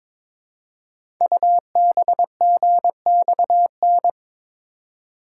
Morse code, heard as a keyed tone, about 22 wpm, 700 Hz. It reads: UBGXN
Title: Code Groups - 5 Letters - 22wpm